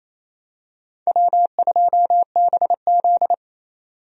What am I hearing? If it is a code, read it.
Morse code, heard as a keyed tone, about 28 wpm, 700 Hz. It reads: W2BZ
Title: USA Call Signs - 4 Characters - 28wpm